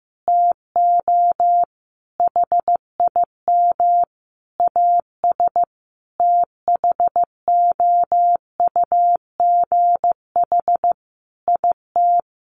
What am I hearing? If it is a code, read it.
Morse code, heard as a keyed tone, about 15 wpm, 700 Hz. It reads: TO HIM AS THOUGH IT
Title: Sentences from Top 1500 Words 15wpm